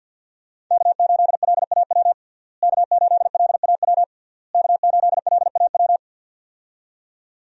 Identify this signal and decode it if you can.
Morse code, heard as a keyed tone, about 51 wpm, 700 Hz. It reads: K8LAW K8LAW K8LAW
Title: USA Call Signs - 5 Characters - Repeated 3x - 50wpm